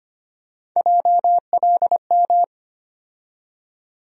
Morse code, heard as a keyed tone, about 25 wpm, 700 Hz. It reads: JLM
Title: USA Call Sign Postfixes - 3 Letters 25wpm